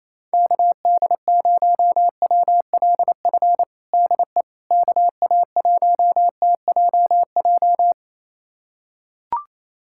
Morse code, heard as a keyed tone, about 28 wpm, 700 Hz. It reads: KD0WLF DE KA1TJJ
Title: USA Callsign DE Callsign - 6 Characters - 28wpm